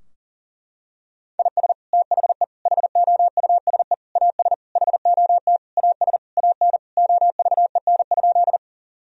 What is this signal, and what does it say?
Morse code, heard as a keyed tone, about 40 wpm, 700 Hz. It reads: IS THE HOUSE AS HOT AS AN OVEN?
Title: Sentences from Top 1000 Words - Review Part 4 - 40wpm